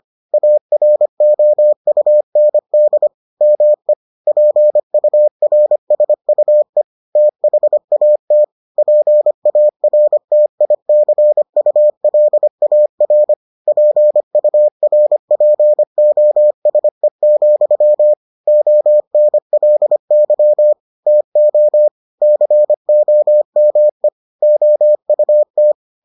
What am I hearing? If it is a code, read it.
Morse code, heard as a keyed tone, about 25 wpm, 600 Hz. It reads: AROUND ME PURSUE THAT PARTICULAR PURPOSE, ONLY TO COME OUT